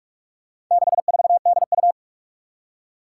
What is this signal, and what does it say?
Morse code, heard as a keyed tone, about 44 wpm, 700 Hz. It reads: 64DU
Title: Four Challenging Letter-Number Combinations 45wpm